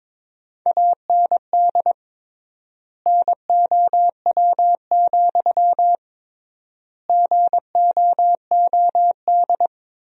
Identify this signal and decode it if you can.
Morse code, heard as a keyed tone, about 22 wpm, 700 Hz. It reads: AND NOW, GOOD
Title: 3x Word-Spacing - Sentences from Top 300 Words - 22wpm